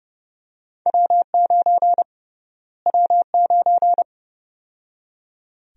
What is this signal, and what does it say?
Morse code, heard as a keyed tone, about 30 wpm, 700 Hz. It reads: W9 W9
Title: USA Call Sign Prefixes - Repeated 2x - 30wpm